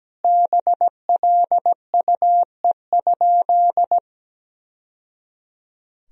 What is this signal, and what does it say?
Morse code, heard as a keyed tone, about 17 wpm, 700 Hz. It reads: BLUE?
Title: Sentences from Top 1000 Words - Review Part 3 - 17wpm